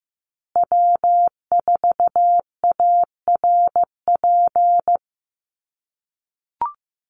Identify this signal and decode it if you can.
Morse code, heard as a keyed tone, about 15 wpm, 700 Hz. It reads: W4ARP
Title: USA Call Signs Repeated 3x 15wpm